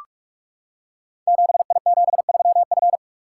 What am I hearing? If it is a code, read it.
Morse code, heard as a keyed tone, about 44 wpm, 700 Hz. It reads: 7I73F
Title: Code Groups - 5 Letter:Numbers - 45wpm